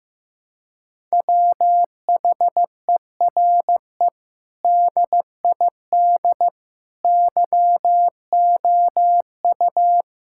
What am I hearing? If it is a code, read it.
Morse code, heard as a keyed tone, about 15 wpm, 700 Hz. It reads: WHERE DID YOU
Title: Sentences from Top 1000 Words - Review Part 6 - 15wpm